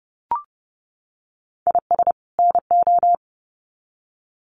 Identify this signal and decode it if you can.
Morse code, heard as a keyed tone, about 30 wpm, 700 Hz. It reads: IS NO
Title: Sets of 2 Words 30wpm